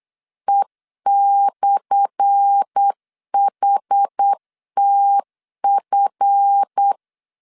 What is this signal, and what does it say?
Morse code, heard as a keyed tone, about 8 wpm, 800 Hz. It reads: E/HTF
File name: LEZIONI / 10 SIGNS / 10A SLASH 8wpm 3452